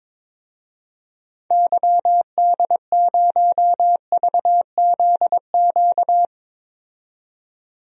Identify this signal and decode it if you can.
Morse code, heard as a keyed tone, about 23 wpm, 700 Hz. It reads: YD0VZQ
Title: DXCC-100 Call Signs - 6 Characters - 22wpm